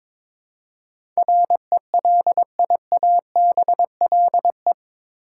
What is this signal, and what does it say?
Morse code, heard as a keyed tone, about 22 wpm, 700 Hz. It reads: RELIABLE
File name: practice / 22wpm / Eight Letter Words - 22wpm